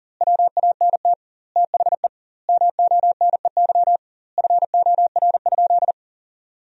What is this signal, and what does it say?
Morse code, heard as a keyed tone, about 37 wpm, 700 Hz. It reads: WANT THE MONEY FOR?
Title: 2x Word-Spacing - Sentences from Top 400 Words - 40wpm